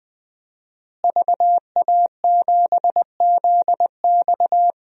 Morse code, heard as a keyed tone, about 21 wpm, 700 Hz. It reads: VA7ZX
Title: DXCC-100 Call Signs - 5 Characters - Repeated 2x - 20wpm